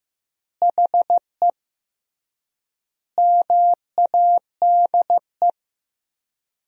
Morse code, heard as a keyed tone, about 16 wpm, 700 Hz. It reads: HE MADE
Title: 3x Word-Spacing - Sentences from Top 300 Words - 15wpm